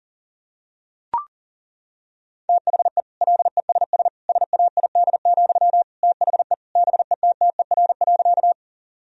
Morse code, heard as a keyed tone, about 40 wpm, 700 Hz. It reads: THE LESS SAID, THE BETTER.
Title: Sentences from Top 1000 Words - Review Part 2 - 40wpm